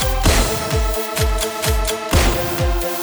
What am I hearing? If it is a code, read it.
There is no code.